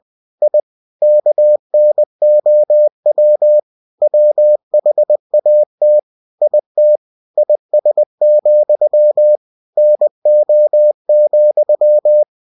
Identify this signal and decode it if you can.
Morse code, heard as a keyed tone, about 20 wpm, 600 Hz. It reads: I KNOW WHAT IT IS, NO,
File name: cw-books / winnie-the-pooh / winnie-20wpm-page-0109